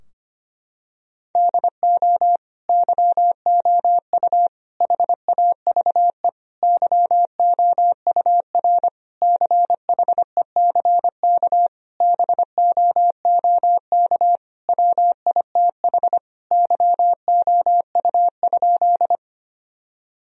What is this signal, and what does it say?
Morse code, heard as a keyed tone, about 25 wpm, 700 Hz. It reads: DO YOU HAVE YOUR CHECK BOOK WITH YOU?
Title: Sentences from Top 1000 Words - Review Part 3 - 25wpm